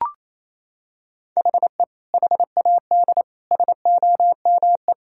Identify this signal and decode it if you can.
Morse code, heard as a keyed tone, about 28 wpm, 700 Hz. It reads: HE HAD SOME
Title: Sentences from Top 1000 Words - Review Part 3 - 28wpm